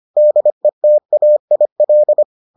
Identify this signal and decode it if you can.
Morse code, heard as a keyed tone, about 25 wpm, 600 Hz. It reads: DETAIL